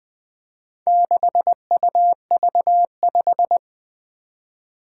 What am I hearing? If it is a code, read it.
Morse code, heard as a keyed tone, about 20 wpm, 700 Hz. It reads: 6UV5
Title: Four Challenging Letter-Number Combinations 20wpm